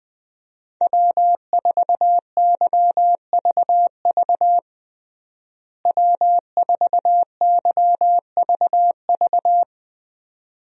Morse code, heard as a keyed tone, about 21 wpm, 700 Hz. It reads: W4YVV W4YVV
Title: USA Call Signs - 5 Characters - Repeated 2x - 20wpm